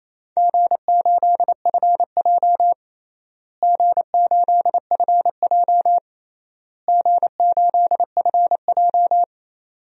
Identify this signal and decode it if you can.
Morse code, heard as a keyed tone, about 28 wpm, 700 Hz. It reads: G8FJ G8FJ G8FJ
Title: DXCC-100 Call Signs - 4 Characters - Repeated 3x - 28wpm